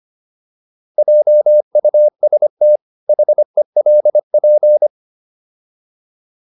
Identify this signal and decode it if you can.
Morse code, heard as a keyed tone, about 25 wpm, 600 Hz. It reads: JUST HELP